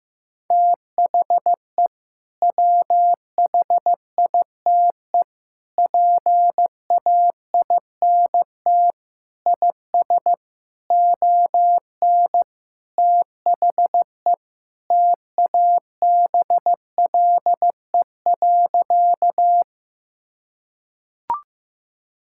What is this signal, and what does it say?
Morse code, heard as a keyed tone, about 15 wpm, 700 Hz. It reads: THE WHITE PAINT IS ON THE TABLE.
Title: Sentences from Top 1000 Words - Review Part 6 - 15wpm